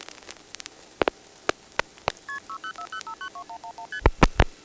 Multi-frequency trunk signalling, heard as fast pairs of tones, tones of about 90 ms. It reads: KP6040592111ST